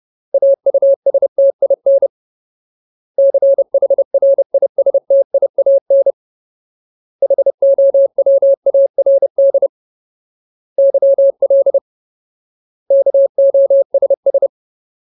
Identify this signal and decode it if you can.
Morse code, heard as a keyed tone, about 30 wpm, 550 Hz. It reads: AUSTIN CHRISTIAN HOWARD YL KOSS
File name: Ultimate Word List / 30WPM Ultimate